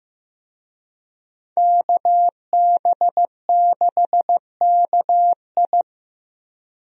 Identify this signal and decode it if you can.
Morse code, heard as a keyed tone, about 16 wpm, 700 Hz. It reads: KB6KI
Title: USA Call Signs - 5 Characters - Repeated 3x - 15wpm